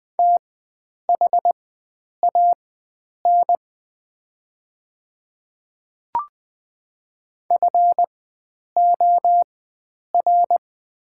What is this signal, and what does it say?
Morse code, heard as a keyed tone, about 20 wpm, spaced as for 9 wpm, 700 Hz. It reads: THAN FOR
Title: 4x Character Spacing - Top 100 Words - 20wpm